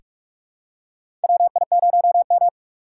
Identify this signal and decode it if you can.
Morse code, heard as a keyed tone, about 42 wpm, 700 Hz. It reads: WI0M
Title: USA Call Signs - 4 Characters - 2x Character Spacing - 45wpm